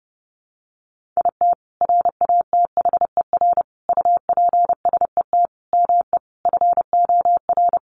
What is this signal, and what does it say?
Morse code, heard as a keyed tone, about 30 wpm, 700 Hz. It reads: IT RATHER UPSET ME FOR
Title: Sentences from Top 900 Words 30wpm